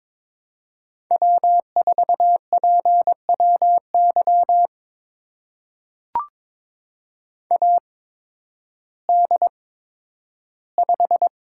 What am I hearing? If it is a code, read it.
Morse code, heard as a keyed tone, about 22 wpm, spaced as for 12 wpm, 700 Hz. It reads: W4PWY AD5
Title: USA Call Signs - 5 Characters - 8x Character Spacing - 22wpm